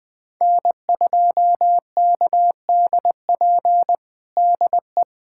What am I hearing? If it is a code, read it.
Morse code, heard as a keyed tone, about 20 wpm, 700 Hz. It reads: N2KDP DE
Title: USA Callsign DE Callsign - 5 Characters - 20wpm